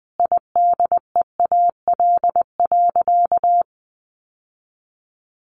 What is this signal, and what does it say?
Morse code, heard as a keyed tone, about 20 wpm, 700 Hz. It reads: IDEAL.
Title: Sentences from Top 1400 Words 20wpm